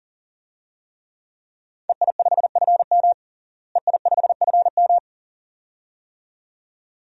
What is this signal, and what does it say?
Morse code, heard as a keyed tone, about 37 wpm, 700 Hz. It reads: EI5FM EI5FM
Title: DXCC-100 Call Signs - 5 Characters - Repeated 2x - 40wpm